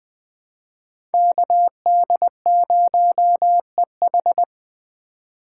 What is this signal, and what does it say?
Morse code, heard as a keyed tone, about 20 wpm, 700 Hz. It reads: KD0EH
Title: USA Call Signs - 5 Characters - 3x Character Spacing - 20wpm